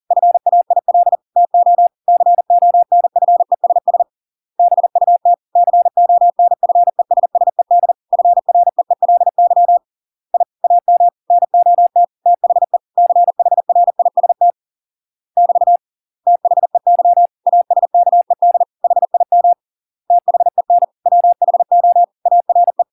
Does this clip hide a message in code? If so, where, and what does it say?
Yes: Morse code from 0.1 to 22.8 s, heard as a keyed tone, about 40 wpm, 700 Hz. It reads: FAIL TO CONFESS BUT CONFESSED FREELY I AM NOT THE CHRIST <BT> THEY ASKED HIM THEN WHO ARE